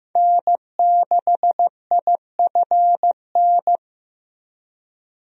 Morse code, heard as a keyed tone, about 15 wpm, 700 Hz. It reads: N6IFN
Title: USA Call Signs - 5 Characters - 15wpm